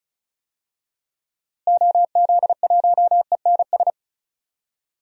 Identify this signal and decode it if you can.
Morse code, heard as a keyed tone, about 35 wpm, 700 Hz. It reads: OZ1ENS
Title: DXCC-100 Call Signs - 6 Characters - Repeated 2x - 35wpm